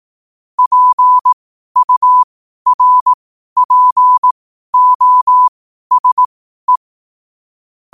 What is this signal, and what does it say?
Morse code, heard as a keyed tone, about 18 wpm, spaced as for 12 wpm, 1000 Hz. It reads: PURPOSE
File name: qotdfiles / 12wpm